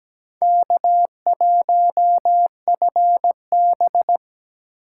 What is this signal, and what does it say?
Morse code, heard as a keyed tone, about 17 wpm, 700 Hz. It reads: K1FB
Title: USA Call Signs - 4 Characters - 6x Character Spacing - 17wpm